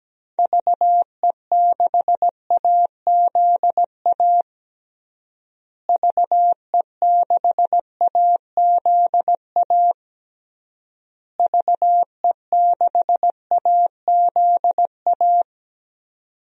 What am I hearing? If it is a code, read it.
Morse code, heard as a keyed tone, about 17 wpm, 700 Hz. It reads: VE6AZA VE6AZA VE6AZA
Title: DXCC-100 Call Signs - 6 Characters - Repeated 3x - 17wpm